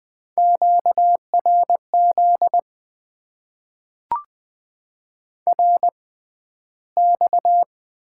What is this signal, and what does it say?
Morse code, heard as a keyed tone, about 20 wpm, spaced as for 12 wpm, 700 Hz. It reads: QRZ RX
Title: Top 90 QSO Elements - 6x Character Spacing - 20wpm